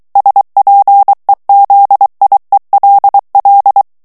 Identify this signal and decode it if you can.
Morse code, heard as a keyed tone, about 23 wpm, 800 Hz. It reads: SPEZIELL
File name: morse mp3 / ALLEMAND texte clair 1 vitesse 22 mots minute